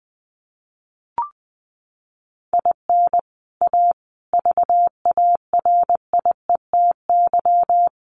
Morse code, heard as a keyed tone, about 20 wpm, 700 Hz. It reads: IN A VARIETY
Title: Sets of 4 Words 20wpm